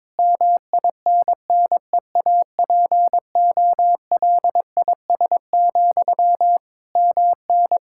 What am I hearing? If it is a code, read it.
Morse code, heard as a keyed tone, about 22 wpm, 700 Hz. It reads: MINNEAPOLIS, MN
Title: Practice QSO - Exchange 1 - Beginning - 22wpm